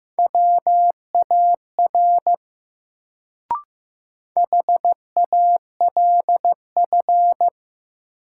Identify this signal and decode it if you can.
Morse code, heard as a keyed tone, about 15 wpm, 700 Hz. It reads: WAR HALF
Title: Sentences from Top 400 Words - New Word Drill - 15wpm